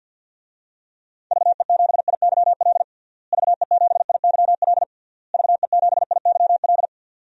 Morse code, heard as a keyed tone, about 52 wpm, 700 Hz. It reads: VE7IYL VE7IYL VE7IYL
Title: DXCC-100 Call Signs - 6 Characters - Repeated 3x - 50wpm